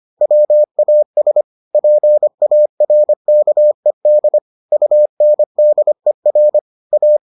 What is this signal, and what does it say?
Morse code, heard as a keyed tone, about 25 wpm, 600 Hz. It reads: WAS PARKED UNDER A